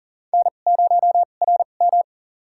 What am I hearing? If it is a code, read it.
Morse code, heard as a keyed tone, about 39 wpm, spaced as for 28 wpm, 700 Hz. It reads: N0RM
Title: USA Call Signs - 2x Character Spacing - 40wpm